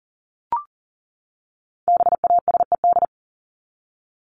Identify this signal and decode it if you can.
Morse code, heard as a keyed tone, about 40 wpm, 700 Hz. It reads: BASED
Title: ED Suffix - Top 100 Words with Base Words 40wpm